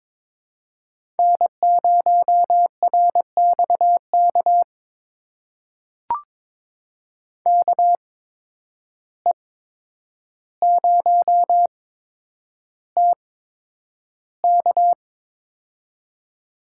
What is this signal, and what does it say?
Morse code, heard as a keyed tone, about 22 wpm, spaced as for 9 wpm, 700 Hz. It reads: N0RXK KE0TK
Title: USA Call Signs - 8x Character Spacing - 22wpm